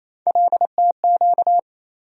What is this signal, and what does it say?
Morse code, heard as a keyed tone, about 28 wpm, 700 Hz. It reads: LTQ
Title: USA Call Sign Postfixes - 3 Letters 28wpm